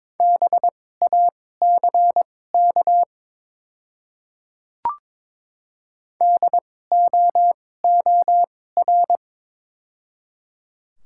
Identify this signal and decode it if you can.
Morse code, heard as a keyed tone, about 22 wpm, spaced as for 16 wpm, 700 Hz. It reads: BACK DOOR
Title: 2x Character Spacing - Top 500 Words - 22wpm